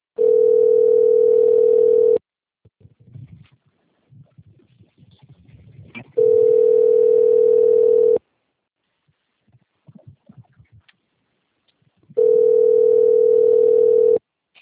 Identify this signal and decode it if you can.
Ringback tone (North America), 440 + 480 Hz.
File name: rg-1600-unknown- - - .29929